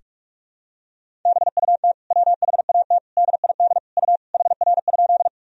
Morse code, heard as a keyed tone, about 42 wpm, 700 Hz. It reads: BUT WHAT DID U HR?
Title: Sentences from Top 300 Words and Abbreviations 45wpm